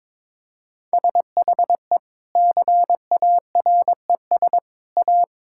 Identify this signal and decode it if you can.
Morse code, heard as a keyed tone, about 22 wpm, 700 Hz. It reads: SHE CARES A
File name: practice / 22wpm / Sentences with 3-Word Phrases - 22wpm